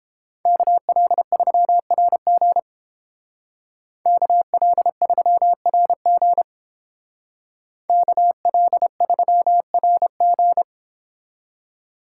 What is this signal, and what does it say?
Morse code, heard as a keyed tone, about 29 wpm, 700 Hz. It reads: KL3RG KL3RG KL3RG